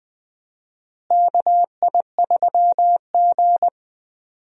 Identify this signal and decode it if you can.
Morse code, heard as a keyed tone, about 20 wpm, 700 Hz. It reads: KI3G
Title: USA Call Signs - 4 Characters - Repeated 3x - 20wpm